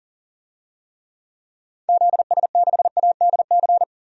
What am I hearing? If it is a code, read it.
Morse code, heard as a keyed tone, about 40 wpm, 700 Hz. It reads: ZS6ADC
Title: DXCC-100 Call Signs - 6 Characters - 40wpm